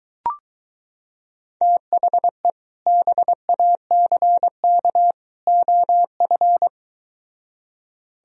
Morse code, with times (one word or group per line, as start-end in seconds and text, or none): 1.6-2.5 THE
2.9-5.1 BACK
5.5-6.7 OF